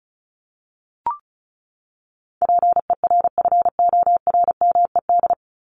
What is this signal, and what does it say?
Morse code, heard as a keyed tone, about 34 wpm, 700 Hz. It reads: PERFORMED
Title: ED Suffix - Top 100 Words with Base Words 35wpm